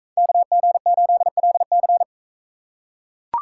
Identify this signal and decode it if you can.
Morse code, heard as a keyed tone, about 41 wpm, 700 Hz. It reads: KG8LC